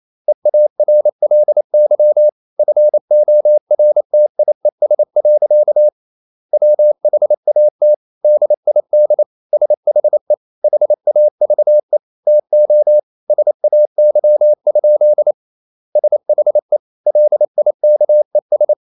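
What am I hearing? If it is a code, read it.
Morse code, heard as a keyed tone, about 28 wpm, 600 Hz. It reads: EARLY FORTIES. WHAT DID SHE HAVE TO SAY? SHE LIKES
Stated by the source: Morse code practice at 28 wpm, text of Ken Follett's novel Never in English, from CW med Gnister.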